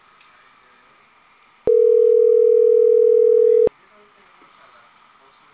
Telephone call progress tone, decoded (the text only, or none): ringback tone